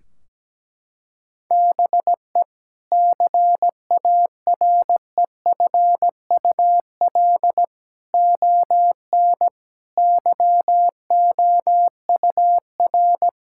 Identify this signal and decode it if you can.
Morse code, heard as a keyed tone, about 17 wpm, 700 Hz. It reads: BE CAREFUL ON YOUR
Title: Sentences from Top 1600 Words 17wpm